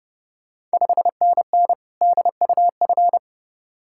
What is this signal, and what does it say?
Morse code, heard as a keyed tone, about 30 wpm, 690 Hz. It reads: 5NN DUF
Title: Ontario QSO Party Exchange 30wpm